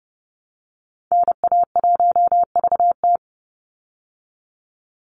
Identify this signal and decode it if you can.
Morse code, heard as a keyed tone, about 30 wpm, 700 Hz. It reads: NA1VT
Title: Call Signs Repeated 3x - 30wpm